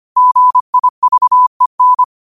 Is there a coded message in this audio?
Yes: Morse code from 0.2 to 2.0 s, heard as a keyed tone, about 25 wpm, 1000 Hz. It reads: GIVEN